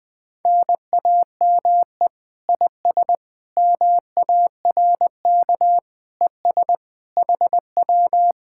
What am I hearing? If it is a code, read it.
Morse code, heard as a keyed tone, about 20 wpm, 700 Hz. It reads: NAME IS MARK ES HW
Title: Speed-Racing Common QSO Phrases - 17wpm